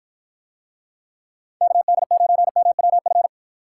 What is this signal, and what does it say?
Morse code, heard as a keyed tone, about 47 wpm, 700 Hz. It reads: KD9MWF